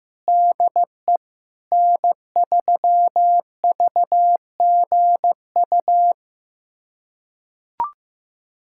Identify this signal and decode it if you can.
Morse code, heard as a keyed tone, about 15 wpm, 700 Hz. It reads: DE N3VGU
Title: USA Callsign DE Callsign - 5 Characters - 15wpm